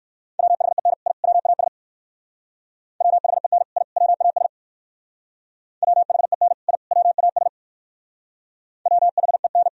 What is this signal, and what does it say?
Morse code, heard as a keyed tone, about 59 wpm, 700 Hz. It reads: WHEN I WAS WHEN I WAS WHEN I WAS WHEN